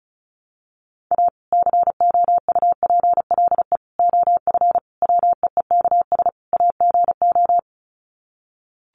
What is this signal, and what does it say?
Morse code, heard as a keyed tone, about 34 wpm, 700 Hz. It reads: A COUPLE OF WEEKS AGO
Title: Sets of 5 Words 35wpm